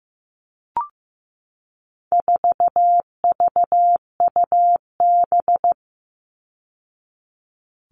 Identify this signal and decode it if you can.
Morse code, heard as a keyed tone, about 15 wpm, 700 Hz. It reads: 4VUB
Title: All Cut Numbers - 4-Digit 15wpm